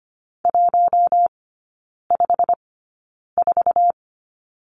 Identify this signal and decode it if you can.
Morse code, heard as a keyed tone, about 25 wpm, spaced as for 9 wpm, 700 Hz. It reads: 154
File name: Lesson 15e - Number 5 - Three Characters 25wpm